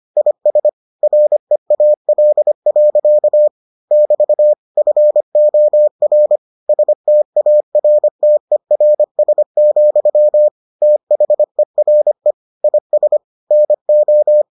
Morse code, heard as a keyed tone, about 25 wpm, 600 Hz. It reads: IS REAL. <BT> FOR STARTERS, THERE IS NO